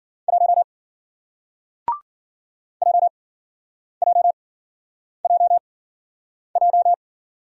Morse code, heard as a keyed tone, about 53 wpm, spaced as for 9 wpm, 700 Hz. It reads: 1 JJJJ